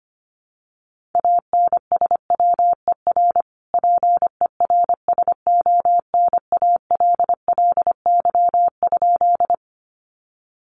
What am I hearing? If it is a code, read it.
Morse code, heard as a keyed tone, about 25 wpm, 700 Hz. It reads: ANSWER PERSONALLY?
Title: Sentences from Top 1400 Words 25wpm